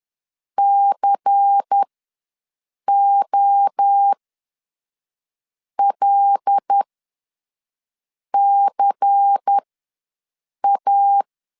Morse code, heard as a keyed tone, about 10 wpm, spaced as for 5 wpm, 800 Hz. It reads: COLCA
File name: LEZIONI / 5 LBVC / 5 LBVC ws 10wpm 3733